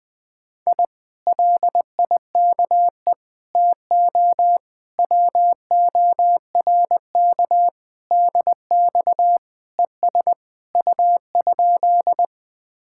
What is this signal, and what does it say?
Morse code, heard as a keyed tone, about 20 wpm, 700 Hz. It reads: I LIKE TO WORK DX ES U?
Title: Speed-Racing Common QSO Phrases - 17wpm